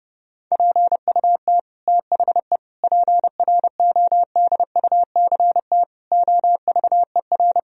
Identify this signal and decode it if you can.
Morse code, heard as a keyed tone, about 32 wpm, 700 Hz. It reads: PUT THE PRODUCT OVER
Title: Sentences from Top 1000 Words - Review Part 2 - 30wpm